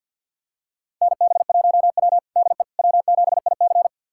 Morse code, heard as a keyed tone, about 50 wpm, 700 Hz. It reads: NB1W DE W7IC
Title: USA Callsign DE Callsign - 4 Characters - 50wpm